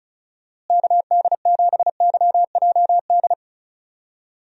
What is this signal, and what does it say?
Morse code, heard as a keyed tone, about 36 wpm, 700 Hz. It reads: KD7YJD
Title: USA Call Signs - 6 Characters - Repeated 2x - 35wpm